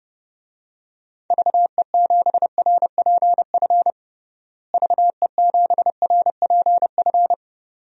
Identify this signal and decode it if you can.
Morse code, heard as a keyed tone, about 30 wpm, 700 Hz. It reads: VE7RPF VE7RPF
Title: DXCC-100 Call Signs - 6 Characters - Repeated 2x - 30wpm